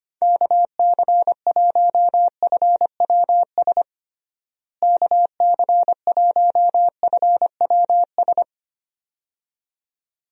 Morse code, heard as a keyed tone, about 25 wpm, 700 Hz. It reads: KC1FWS KC1FWS
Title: USA Call Signs - 6 Characters - Repeated 2x - 25wpm